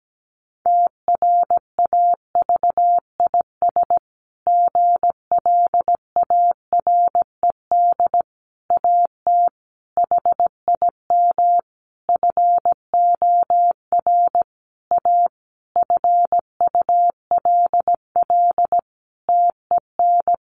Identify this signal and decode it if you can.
Morse code, heard as a keyed tone, about 17 wpm, 700 Hz. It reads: TRAVIS GLARED AT HIM FOR A FULL TEN
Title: A Sound Of Thunder 17wpm